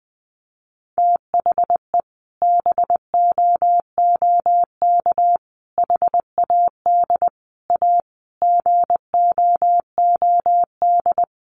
Morse code, heard as a keyed tone, about 20 wpm, 700 Hz. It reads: THE BOOK HAD A GOOD
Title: Sentences from Top 1400 Words 20wpm